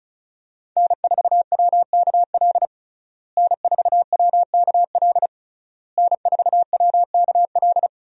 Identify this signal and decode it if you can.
Morse code, heard as a keyed tone, about 34 wpm, 700 Hz. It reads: N4WKL N4WKL N4WKL
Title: USA Call Signs - 5 Characters - Repeated 3x - 35wpm